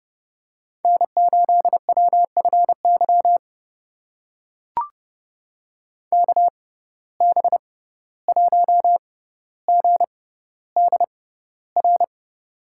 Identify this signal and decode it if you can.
Morse code, heard as a keyed tone, about 30 wpm, spaced as for 15 wpm, 700 Hz. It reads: N8WFY KB1GDR
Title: USA Call Signs - 6x Character Spacing - 30wpm